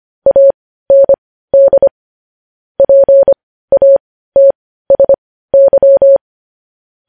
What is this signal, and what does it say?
Morse code, heard as a keyed tone, about 25 wpm, spaced as for 15 wpm, 550 Hz. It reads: AND PATSY
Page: news-headlines-in-morse-code-at-15-wpm